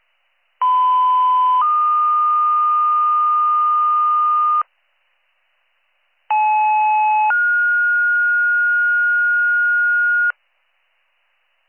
Two-tone page, heard as a one-second tone, then a three-second tone, about 1010 then 1220 Hz, about 860 then 1430 Hz. It is heard from 0.6 to 10.3 s.